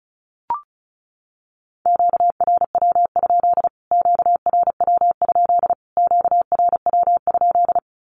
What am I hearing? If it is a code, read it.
Morse code, heard as a keyed tone, about 34 wpm, 700 Hz. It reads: QRW? QRW? QRW?
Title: Q-Codes Repeated 3x 35wpm